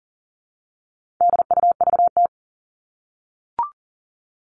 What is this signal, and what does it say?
Morse code, heard as a keyed tone, about 38 wpm, 700 Hz. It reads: DUVT
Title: All Cut Numbers - 4-Digit 40wpm